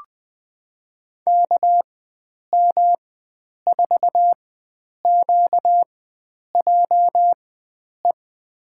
Morse code, heard as a keyed tone, about 20 wpm, spaced as for 9 wpm, 700 Hz. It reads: KM4QJE
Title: USA Call Signs - 6 Characters - 4x Character Spacing - 20wpm